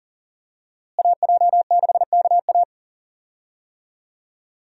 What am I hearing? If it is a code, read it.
Morse code, heard as a keyed tone, about 40 wpm, 700 Hz. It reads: AJ6KA
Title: USA Call Signs - 5 Characters - Repeated 2x - 40wpm